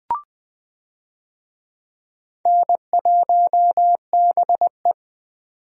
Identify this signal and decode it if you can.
Morse code, heard as a keyed tone, about 20 wpm, 700 Hz. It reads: N1BE
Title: USA Call Signs - 4 Characters - 20wpm